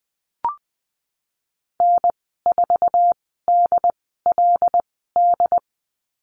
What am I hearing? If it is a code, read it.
Morse code, heard as a keyed tone, about 20 wpm, spaced as for 15 wpm, 700 Hz. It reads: N4DLD
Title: 161 Lesson 26g - Letter g - Call Signs 20wpm